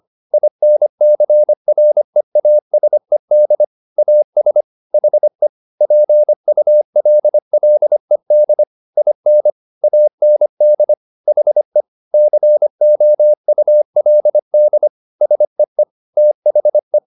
Morse code, heard as a keyed tone, about 25 wpm, 600 Hz. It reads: INCREASED AS HE PULLED IN AND HE COULD SEE THE
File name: the old man and the sea part 1 25wpm